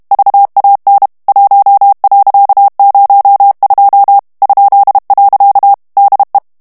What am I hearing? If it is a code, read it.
Morse code, heard as a keyed tone, about 32 wpm, 800 Hz. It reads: VAN 1.02 ?. DE
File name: HOLLANDAIS texte clair 1 vitesse 30 mots minute